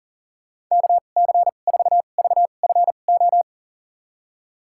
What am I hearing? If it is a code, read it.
Morse code, heard as a keyed tone, about 38 wpm, spaced as for 28 wpm, 700 Hz. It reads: KC4VFO
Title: USA Call Signs - 6 Characters - 2x Character Spacing - 40wpm